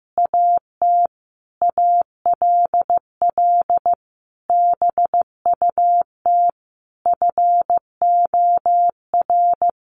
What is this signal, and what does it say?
Morse code, heard as a keyed tone, about 15 wpm, 700 Hz. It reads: AT ALL BUT FOR
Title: Sentences from Top 1400 Words 15wpm